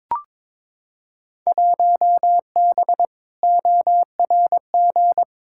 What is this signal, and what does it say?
Morse code, heard as a keyed tone, about 22 wpm, 700 Hz. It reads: 1B ORG
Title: Field Day Exchanges Encore - 22wpm